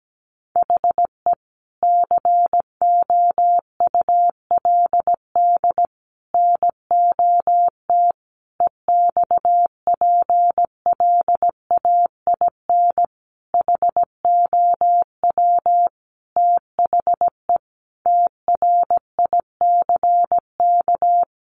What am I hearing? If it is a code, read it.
Morse code, heard as a keyed tone, about 17 wpm, 700 Hz. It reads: HE COULD NOT EXPLAIN HOW THE TRICK
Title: Sentences from Top 1500 Words 17wpm